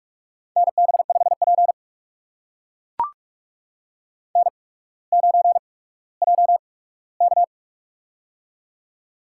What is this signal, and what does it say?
Morse code, heard as a keyed tone, about 44 wpm, spaced as for 19 wpm, 700 Hz. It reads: NB5P N9JK
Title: USA Call Signs - 8x Character Spacing - 45wpm